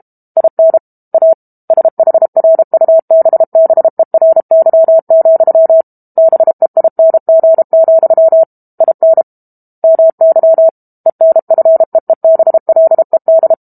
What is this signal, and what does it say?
Morse code, heard as a keyed tone, about 32 wpm, 650 Hz. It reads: IN A SHRUBBERY, BEING, IN MY ENFEEBLED
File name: Book Two / 06-Chapter 6 - 32 WPM